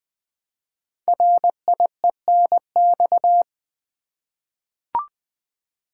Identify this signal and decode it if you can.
Morse code, heard as a keyed tone, about 20 wpm, 700 Hz. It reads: RIENX
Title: Code Groups - 5 Letters - 20wpm